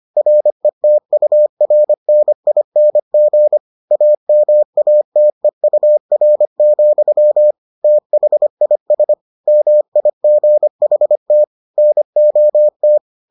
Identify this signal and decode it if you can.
Morse code, heard as a keyed tone, about 25 wpm, 600 Hz. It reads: RETURNING AMATEUR, THIS MIGHT NOT